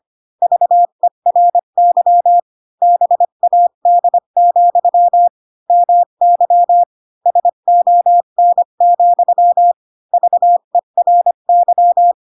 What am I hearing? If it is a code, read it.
Morse code, heard as a keyed tone, about 25 wpm, 700 Hz. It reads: VERY BAD, MY SON, VERY
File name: Pinocchio / Pinocchio ch3 20 25 ik1hge